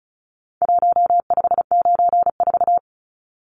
Morse code, heard as a keyed tone, about 34 wpm, 700 Hz. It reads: 1594
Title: Numbers - Contest Style - 4-Digit 35wpm